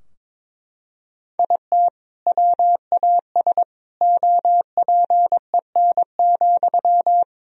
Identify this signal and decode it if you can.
Morse code, heard as a keyed tone, about 21 wpm, 700 Hz. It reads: IT WAS OPEN,
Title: Sentences from Top 1000 Words - Review Part 4 - 22wpm